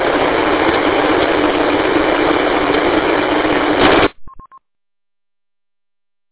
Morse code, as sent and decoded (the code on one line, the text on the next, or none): none